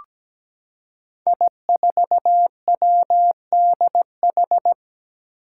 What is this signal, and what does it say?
Morse code, heard as a keyed tone, about 17 wpm, 700 Hz. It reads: I4WDH
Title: Five Challenging Letter-Number Combinations 17wpm